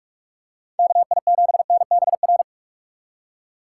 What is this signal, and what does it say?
Morse code, heard as a keyed tone, about 44 wpm, 700 Hz. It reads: KI7NBR
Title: USA Call Signs - 6 Characters - 6x Character Spacing - 45wpm